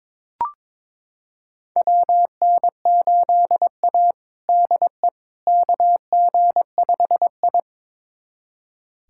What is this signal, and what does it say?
Morse code, heard as a keyed tone, about 22 wpm, 700 Hz. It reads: WN8A DE KG5I
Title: USA Callsign DE Callsign - 4 Characters - 22wpm